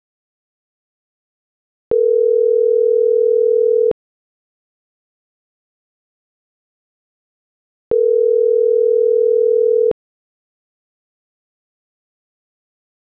Ringback tone (North America), 440 + 480 Hz.